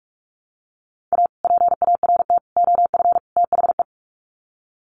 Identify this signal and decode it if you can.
Morse code, heard as a keyed tone, about 44 wpm, 700 Hz. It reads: A PART OF THE
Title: Sets of 4 Words 45wpm